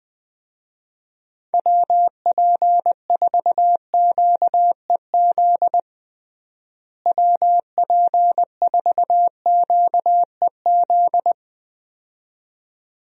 Morse code, heard as a keyed tone, about 21 wpm, 700 Hz. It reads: WP4QEZ WP4QEZ
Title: USA Call Signs - 6 Characters - Repeated 2x - 20wpm